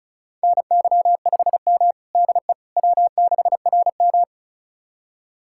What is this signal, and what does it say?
Morse code, heard as a keyed tone, about 36 wpm, 700 Hz. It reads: NY5M DE W6RM
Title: USA Callsign DE Callsign - 4 Characters - 35wpm